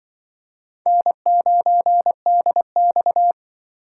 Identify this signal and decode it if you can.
Morse code, heard as a keyed tone, about 25 wpm, 700 Hz. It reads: N9DX